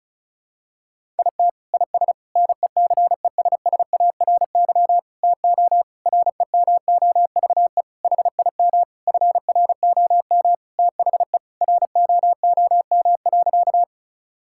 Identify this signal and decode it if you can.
Morse code, heard as a keyed tone, about 34 wpm, 700 Hz. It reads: IT IS NECESSARY TO REMOVE HIM FROM THE ROOM.
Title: Sentences from Top 1000 Words - Review Part 5 - 35wpm